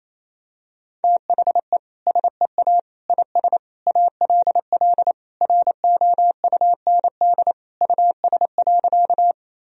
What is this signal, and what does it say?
Morse code, heard as a keyed tone, about 28 wpm, 700 Hz. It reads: THE SEA IS ALL ROUND US.
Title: Sentences from Top 1000 Words - Review Part 1 - 28wpm